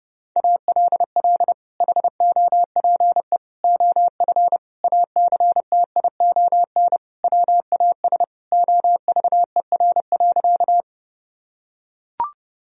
Morse code, heard as a keyed tone, about 30 wpm, 700 Hz. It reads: ALL HOPE OF ACTION WAS OVER.
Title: Sentences from Top 1000 Words - Review Part 4 - 30wpm